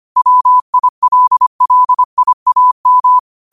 Morse code, heard as a keyed tone, about 25 wpm, 1000 Hz. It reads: WILLIAM
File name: qotdfiles / 25wpm